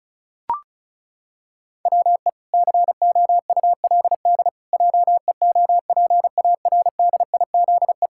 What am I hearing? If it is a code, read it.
Morse code, heard as a keyed tone, about 36 wpm, 700 Hz. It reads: WE COULD JEOPARDIZE
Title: Pangram Challenge v2 - 35wpm